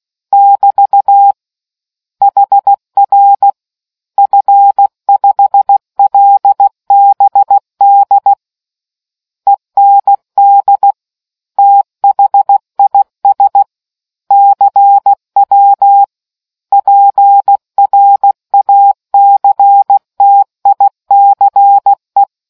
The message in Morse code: <BT> HR F5LBD END THIS CW PRACTICE

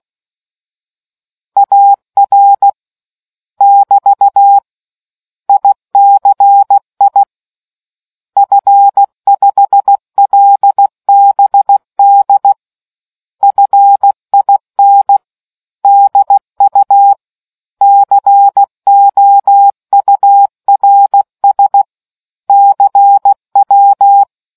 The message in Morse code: AR <BT> ICI F5LBD FIN DU COURS CW

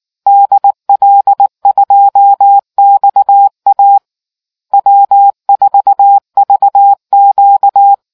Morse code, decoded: DL2XA W4VQ

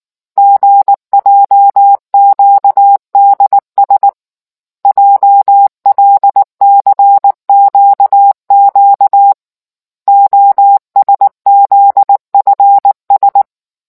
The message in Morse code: GJQDS JLCQQ OSZFS

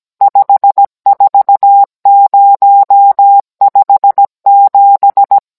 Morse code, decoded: 54057

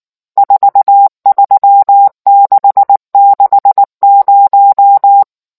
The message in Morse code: 43660

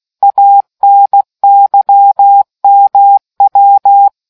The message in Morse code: ANYMW

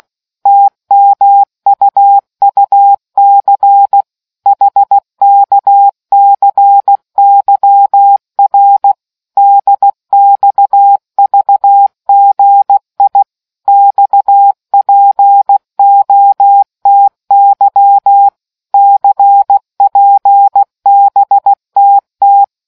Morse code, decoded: TMUUC HKCYR DXVGI XPOTY CPBTT

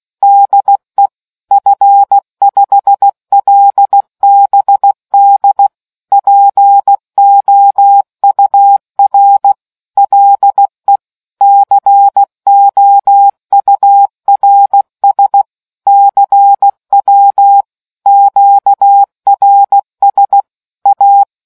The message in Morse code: DE F5LBD POUR LE COURS CW QRS A

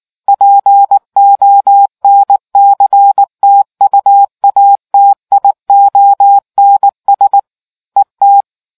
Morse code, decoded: PONCTUATIONS ET

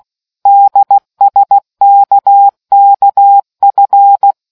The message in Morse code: DSKKF

